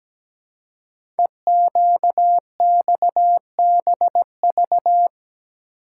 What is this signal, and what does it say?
Morse code, heard as a keyed tone, about 17 wpm, 700 Hz. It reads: EQXBV
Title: Code Groups - 5 Letter:Numbers - 17wpm